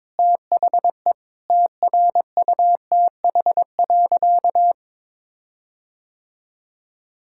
Morse code, heard as a keyed tone, about 22 wpm, 700 Hz. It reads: THE TRUTH.
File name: Sentences with 7-Word Phrases - 22wpm